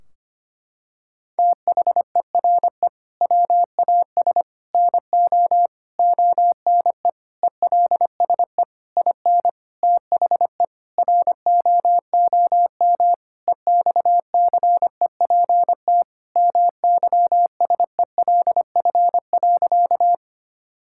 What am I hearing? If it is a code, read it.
Morse code, heard as a keyed tone, about 25 wpm, 700 Hz. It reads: THERE WAS NO ONE ELSE IN THE ROOM EXCEPT MYSELF.
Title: Sentences from Top 1600 Words 25wpm